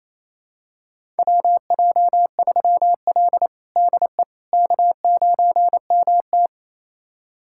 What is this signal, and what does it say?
Morse code, heard as a keyed tone, about 28 wpm, 700 Hz. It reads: WJ3L DE K9MT
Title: USA Callsign DE Callsign - 4 Characters - 28wpm